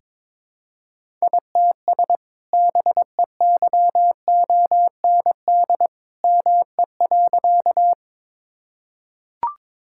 Morse code, heard as a keyed tone, about 21 wpm, 700 Hz. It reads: ITS BEYOND ME.
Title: Sentences from Top 1000 Words - Review Part 6 - 22wpm